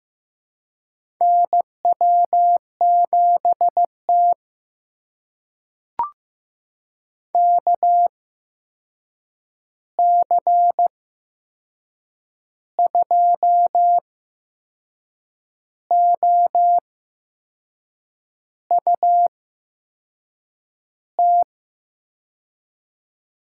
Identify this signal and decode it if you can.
Morse code, heard as a keyed tone, about 15 wpm, spaced as for 6 wpm, 700 Hz. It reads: NW7T KC2OUT
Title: USA Call Signs - 8x Character Spacing - 15wpm